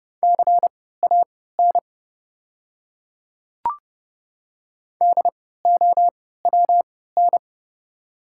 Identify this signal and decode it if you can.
Morse code, heard as a keyed tone, about 30 wpm, spaced as for 17 wpm, 690 Hz. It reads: CAN DOWN
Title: 3x Character Spacing - Top 500 Words - 30wpm